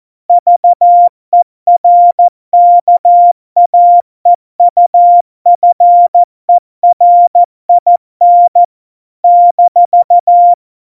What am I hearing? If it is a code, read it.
Morse code, heard as a keyed tone, about 14 wpm, 700 Hz. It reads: VERKAEUFERIN -
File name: Morsekurs / Witze / Witz 03 April 2018, 70 F80